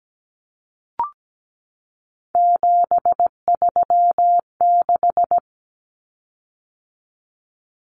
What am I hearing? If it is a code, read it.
Morse code, heard as a keyed tone, about 17 wpm, 700 Hz. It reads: 736
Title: Common Cut Numbers - 3-Digit 17wpm